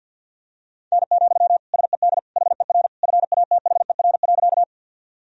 Morse code, heard as a keyed tone, about 53 wpm, 700 Hz. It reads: N, SED HER FATHER.
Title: Sentences from Top 300 Words and Abbreviations 50wpm